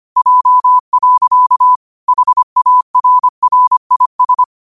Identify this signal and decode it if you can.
Morse code, heard as a keyed tone, about 25 wpm, 1000 Hz. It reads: J. HARRIS
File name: qotdfiles / 25wpm